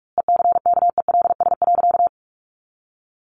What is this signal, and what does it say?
Morse code, heard as a keyed tone, about 44 wpm, 700 Hz. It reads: ECKELS.
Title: A Sound Of Thunder 45wpm